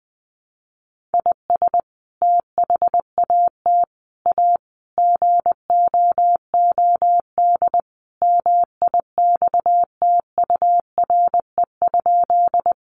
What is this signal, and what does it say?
Morse code, heard as a keyed tone, about 20 wpm, 700 Hz. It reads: IS THAT A GOOD MIXTURE?
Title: Sentences from Top 1500 Words 20wpm